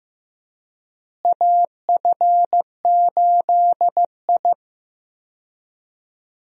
Morse code, heard as a keyed tone, about 15 wpm, 700 Hz. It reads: AF8I
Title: USA Call Signs - 4 Characters - Repeated 3x - 15wpm